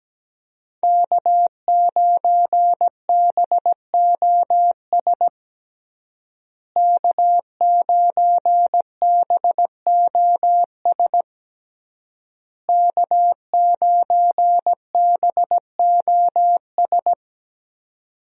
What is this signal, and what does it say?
Morse code, heard as a keyed tone, about 17 wpm, 700 Hz. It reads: K9BOS K9BOS K9BOS
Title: USA Call Signs - 5 Characters - Repeated 3x - 17wpm